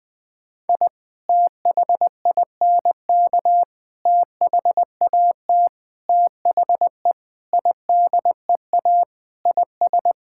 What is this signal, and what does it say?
Morse code, heard as a keyed tone, about 21 wpm, 700 Hz. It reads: I THINK THAT THE IDEA IS
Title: Sentences with 4-Word Phrases - 20wpm